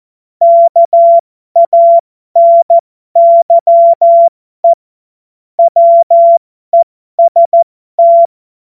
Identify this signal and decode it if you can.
Morse code, heard as a keyed tone, about 14 wpm, spaced as for 12 wpm, 680 Hz. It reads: KANYE WEST
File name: Morsekurs / Witze